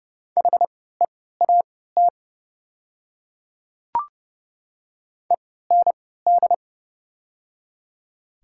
Morse code, heard as a keyed tone, about 31 wpm, spaced as for 17 wpm, 700 Hz. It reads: HEAT END
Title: 3x Character Spacing - Top 500 Words - 30wpm